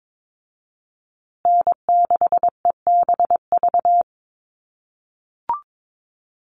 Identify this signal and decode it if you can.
Morse code, heard as a keyed tone, about 22 wpm, 700 Hz. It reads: N6EBV
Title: USA Call Signs Repeated 3x 22wpm